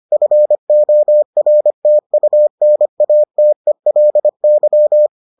Morse code, heard as a keyed tone, about 25 wpm, 600 Hz. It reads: FORTUNATELY